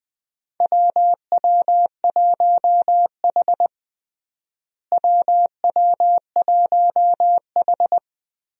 Morse code, heard as a keyed tone, about 21 wpm, 700 Hz. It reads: WW1H WW1H
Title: USA Call Signs - 4 Characters - Repeated 3x - 20wpm